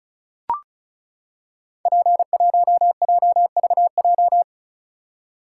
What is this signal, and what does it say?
Morse code, heard as a keyed tone, about 36 wpm, 700 Hz. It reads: P1JVJ
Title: Code Groups - 5 Letter:Numbers - 35wpm